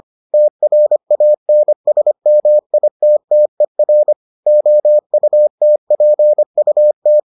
Morse code, heard as a keyed tone, about 25 wpm, 600 Hz. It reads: TRANSMITTER OUTPUT